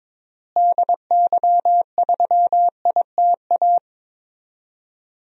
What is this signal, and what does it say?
Morse code, heard as a keyed tone, about 22 wpm, 700 Hz. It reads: DY3ITA
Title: DXCC-100 Call Signs - 6 Characters - Repeated 3x - 22wpm